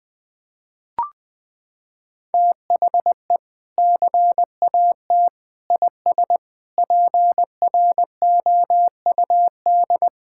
Morse code, heard as a keyed tone, about 21 wpm, 700 Hz. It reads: THE CAT IS PROUD
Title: Sentences with 2-Word Phrases - 20wpm